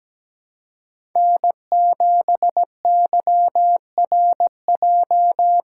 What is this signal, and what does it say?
Morse code, heard as a keyed tone, about 17 wpm, 700 Hz. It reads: N7YRJ
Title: USA Call Signs - 5 Characters - 6x Character Spacing - 17wpm